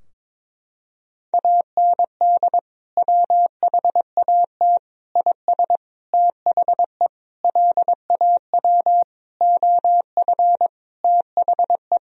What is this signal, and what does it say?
Morse code, heard as a keyed tone, about 22 wpm, 700 Hz. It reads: AND WHAT IS THE LAW OF THE
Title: Sentences from Top 1000 Words - Review Part 3 - 22wpm